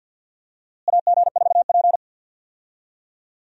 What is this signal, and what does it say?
Morse code, heard as a keyed tone, about 51 wpm, 700 Hz. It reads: AG4P
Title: USA Call Signs - 4 Characters - Repeated 3x - 50wpm